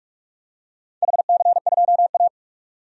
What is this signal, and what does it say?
Morse code, heard as a keyed tone, about 46 wpm, 700 Hz. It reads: HC2A
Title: DXCC-100 Call Signs - 4 Characters - 45wpm